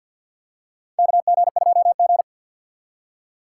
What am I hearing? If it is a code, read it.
Morse code, heard as a keyed tone, about 52 wpm, 700 Hz. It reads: KG2G
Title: USA Call Signs - 4 Characters - 3x Character Spacing - 50wpm